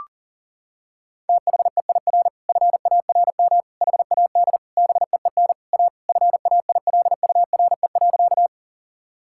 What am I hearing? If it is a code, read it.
Morse code, heard as a keyed tone, about 40 wpm, 700 Hz. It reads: THEIR FARM HAD BEEN A FAILURE.
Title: Sentences from Top 1000 Words - Review Part 6 - 40wpm